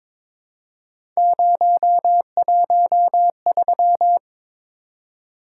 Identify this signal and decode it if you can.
Morse code, heard as a keyed tone, about 22 wpm, 700 Hz. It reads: 013
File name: practice / 22wpm / Three Letter-Number Combinations - 22wpm